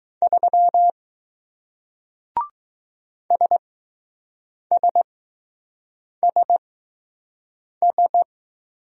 Morse code, heard as a keyed tone, about 20 wpm, spaced as for 6 wpm, 700 Hz. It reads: 3 SSSS